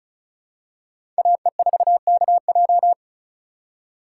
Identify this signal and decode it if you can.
Morse code, heard as a keyed tone, about 34 wpm, 700 Hz. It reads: AE4KJ
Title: USA Call Signs - 5 Characters - 2x Character Spacing - 35wpm